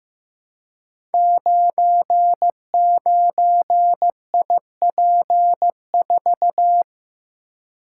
Morse code, heard as a keyed tone, about 15 wpm, 700 Hz. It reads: 99IP4
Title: Code Groups - 5 Letter:Numbers - 15wpm